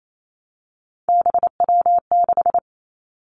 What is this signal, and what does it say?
Morse code, heard as a keyed tone, about 28 wpm, 700 Hz. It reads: BW6
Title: Three Challenging Letter-Number Combinations 28wpm